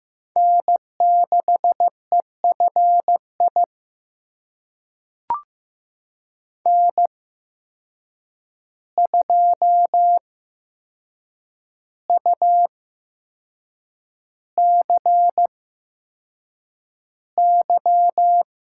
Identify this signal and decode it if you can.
Morse code, heard as a keyed tone, about 15 wpm, spaced as for 6 wpm, 700 Hz. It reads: N6EFI N2UCY
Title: USA Call Signs - 5 Characters - 8x Character Spacing - 15wpm